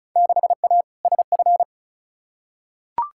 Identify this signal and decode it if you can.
Morse code, heard as a keyed tone, about 34 wpm, 700 Hz. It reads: 6A SF
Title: Field Day Exchanges - 35wpm